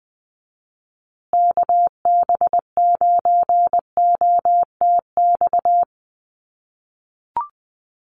Morse code, heard as a keyed tone, about 20 wpm, 700 Hz. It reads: KB9OTX
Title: USA Call Signs Repeated 2x 20wpm